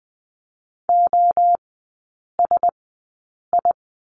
Morse code, heard as a keyed tone, about 20 wpm, spaced as for 8 wpm, 700 Hz. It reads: OSI
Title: 035 Lesson 7e - Letter s - Three Characters 20wpm